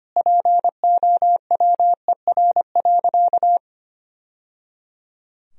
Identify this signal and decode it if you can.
Morse code, heard as a keyed tone, about 25 wpm, 700 Hz. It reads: POWER.
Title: Sentences from Top 1000 Words - Review Part 2 - 25wpm